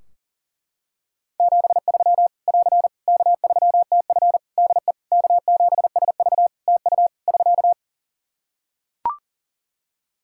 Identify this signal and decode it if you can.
Morse code, heard as a keyed tone, about 40 wpm, 700 Hz. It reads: 73 <AR> K3TF DE K7SV TU <SK>
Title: Practice QSO - Exchange 4 - Ending - 40wpm